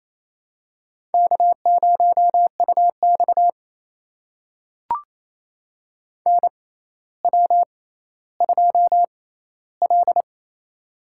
Morse code, heard as a keyed tone, about 28 wpm, spaced as for 14 wpm, 700 Hz. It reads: K0UX NW2L
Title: USA Call Signs - 4 Characters - 6x Character Spacing - 28wpm